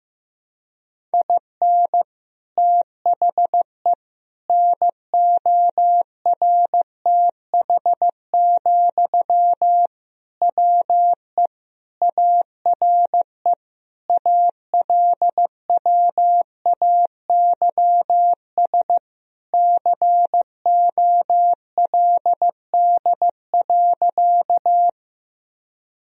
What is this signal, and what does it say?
Morse code, heard as a keyed tone, about 15 wpm, 700 Hz. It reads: IN THE NORTH, WE ARE ALWAYS COLD.
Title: Sentences from Top 1000 Words - Review Part 1 - 15wpm